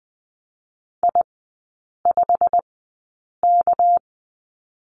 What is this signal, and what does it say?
Morse code, heard as a keyed tone, about 20 wpm, spaced as for 8 wpm, 700 Hz. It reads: I5K
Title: 205 Lesson 33e - Letter j - Three Characters 20wpm